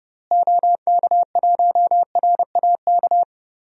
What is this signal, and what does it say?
Morse code, heard as a keyed tone, about 29 wpm, 700 Hz. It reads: OK1RAK
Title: DXCC-100 Call Signs - 6 Characters - Repeated 3x - 30wpm